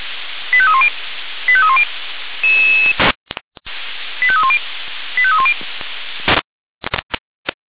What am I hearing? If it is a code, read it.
Five-tone selective call (ZVEI), heard as quick runs of five tones, tones of about 70 ms. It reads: 85310 85310 85310 85310